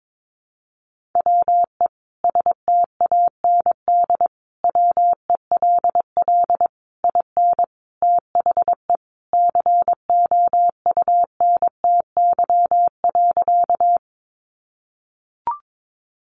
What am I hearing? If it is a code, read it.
Morse code, heard as a keyed tone, about 22 wpm, 700 Hz. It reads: WE STAND WELL IN THE COUNTY.
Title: Sentences from Top 1200 Words 22wpm